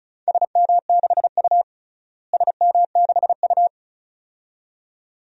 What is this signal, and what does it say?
Morse code, heard as a keyed tone, about 36 wpm, 700 Hz. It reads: SM6U SM6U
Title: DXCC-100 Call Signs - 4 Characters - Repeated 2x - 35wpm